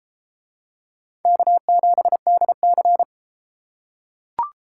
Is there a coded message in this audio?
Yes: Morse code from 1.2 to 3.0 s, heard as a keyed tone, about 33 wpm, 700 Hz. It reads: K7DC